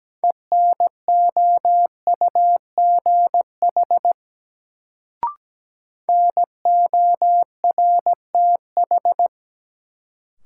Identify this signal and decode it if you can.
Morse code, heard as a keyed tone, about 17 wpm, 700 Hz. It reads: ENOUGH NORTH
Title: Sentences from Top 400 Words - New Word Drill - 17wpm